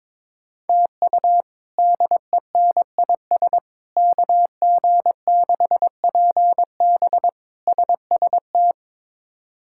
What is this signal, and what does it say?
Morse code, heard as a keyed tone, about 21 wpm, 700 Hz. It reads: TU DENIS KG6PB SST
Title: K1USN SST QSO Practice - 22wpm